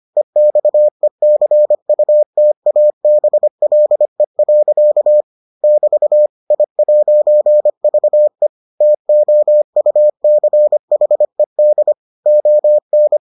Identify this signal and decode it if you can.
Morse code, heard as a keyed tone, about 25 wpm, 600 Hz. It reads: EXECUTABLE. <BT> I'VE TOUCHED ON